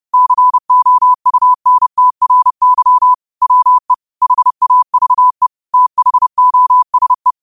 Morse code, heard as a keyed tone, about 30 wpm, 1000 Hz. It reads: COUNTRY WE HAVE THOSE
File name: qotdfiles / 30wpm